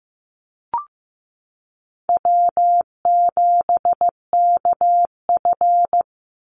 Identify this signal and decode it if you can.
Morse code, heard as a keyed tone, about 15 wpm, 700 Hz. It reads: W7KF
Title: Easy Call Signs - 15wpm